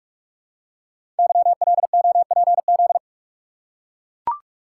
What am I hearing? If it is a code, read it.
Morse code, heard as a keyed tone, about 44 wpm, 690 Hz. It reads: YLOPZ
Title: Code Groups - 5 Letters - 45wpm